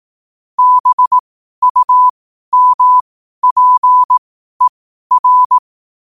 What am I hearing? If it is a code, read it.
Morse code, heard as a keyed tone, about 18 wpm, spaced as for 12 wpm, 1000 Hz. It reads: BUMPER
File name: qotdfiles / 12wpm